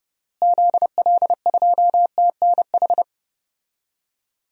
Morse code, heard as a keyed tone, about 30 wpm, 700 Hz. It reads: ZL2TNH
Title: DXCC-100 Call Signs - 6 Characters - Repeated 2x - 30wpm